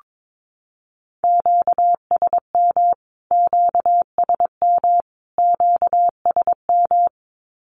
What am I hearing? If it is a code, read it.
Morse code, heard as a keyed tone, about 22 wpm, 700 Hz. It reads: QSM QSM QSM
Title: Q-Codes Repeated 3x 22wpm